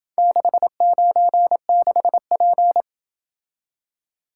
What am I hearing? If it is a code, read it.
Morse code, heard as a keyed tone, about 27 wpm, 700 Hz. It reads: 696P